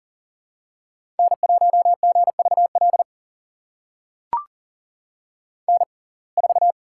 Morse code, heard as a keyed tone, about 38 wpm, spaced as for 28 wpm, 700 Hz. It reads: N1GVL N4
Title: USA Call Signs - 5 Characters - 6x Character Spacing - 40wpm